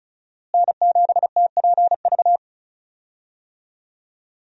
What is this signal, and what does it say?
Morse code, heard as a keyed tone, about 36 wpm, 700 Hz. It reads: N7TPV
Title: USA Call Signs - 5 Characters - Repeated 2x - 35wpm